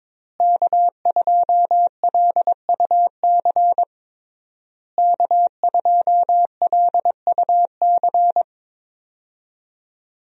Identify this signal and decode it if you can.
Morse code, heard as a keyed tone, about 22 wpm, 700 Hz. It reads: K2LUC K2LUC
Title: USA Call Signs - 5 Characters - Repeated 3x - 22wpm